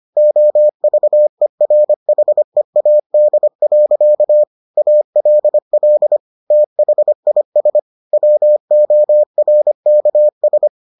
Morse code, heard as a keyed tone, about 25 wpm, 600 Hz. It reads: OVERHEAD. ALL THIS WORKS